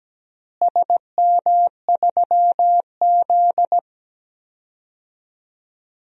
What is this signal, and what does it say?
Morse code, heard as a keyed tone, about 17 wpm, 700 Hz. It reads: SM3Z
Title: DXCC-100 Call Signs - 4 Characters - Repeated 2x - 17wpm